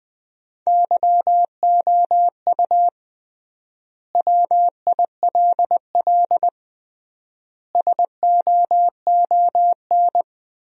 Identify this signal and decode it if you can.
Morse code, heard as a keyed tone, about 20 wpm, 700 Hz. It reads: YOU WILL SOON
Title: 3x Word-Spacing - Sentences from Top 400 Words - 20wpm